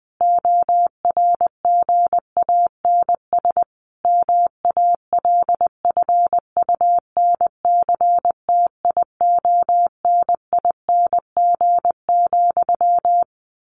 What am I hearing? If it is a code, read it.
Morse code, heard as a keyed tone, about 20 wpm, 700 Hz. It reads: ORGANS MALFUNCTIONING,
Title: A Sound Of Thunder 20wpm